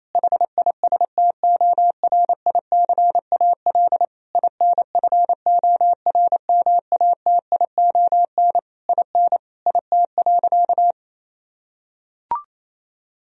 Morse code, heard as a keyed tone, about 28 wpm, 700 Hz. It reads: HISTORICAL INFORMATION IN IT.
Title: Sentences from Top 1000 Words - Review Part 4 - 28wpm